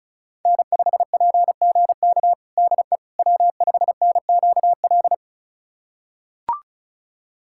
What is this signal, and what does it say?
Morse code, heard as a keyed tone, about 34 wpm, 700 Hz. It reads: N5PGK DE W5NQL
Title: USA Callsign DE Callsign - 5 Characters - 35wpm